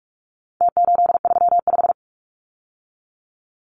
Morse code, heard as a keyed tone, about 44 wpm, 700 Hz. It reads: T835
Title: Numbers - Contest Style - 4-Digit 45wpm